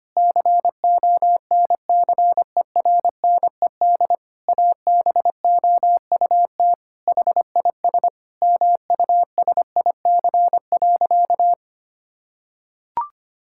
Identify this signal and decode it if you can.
Morse code, heard as a keyed tone, about 25 wpm, 700 Hz. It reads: CONCERNED ABOUT HIS MUSIC.
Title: Sentences from Top 1000 Words - Review Part 6 - 25wpm